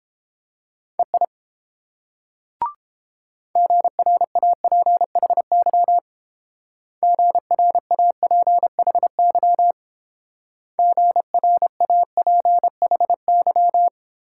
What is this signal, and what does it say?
Morse code, heard as a keyed tone, about 29 wpm, 690 Hz. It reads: EI GRAPHY GRAPHY GRAPHY